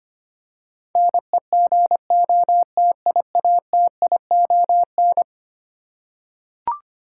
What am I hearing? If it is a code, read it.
Morse code, heard as a keyed tone, about 25 wpm, 700 Hz. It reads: NEGOTIATION
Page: morse-code-ninja-practice-25wpm